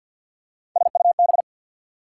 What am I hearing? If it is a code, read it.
Morse code, heard as a keyed tone, about 48 wpm, 690 Hz. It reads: SUB
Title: Prefixes and Suffixes 50wpm